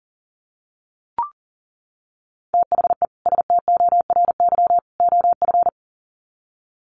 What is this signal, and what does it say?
Morse code, heard as a keyed tone, about 40 wpm, 700 Hz. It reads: THE STORY OF
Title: Sets of 3 Words 40wpm